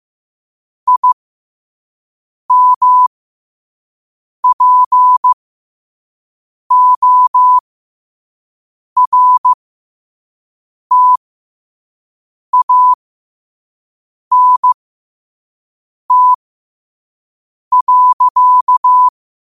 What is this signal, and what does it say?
Morse code, heard as a keyed tone, about 15 wpm, spaced as for 5 wpm, 1000 Hz. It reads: IMPORTANT.
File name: qotdfiles / 5wpm